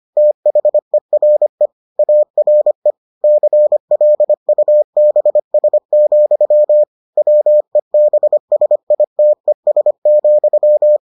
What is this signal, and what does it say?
Morse code, heard as a keyed tone, about 25 wpm, 600 Hz. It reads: THERE ARE CLUBS, WEBSITES,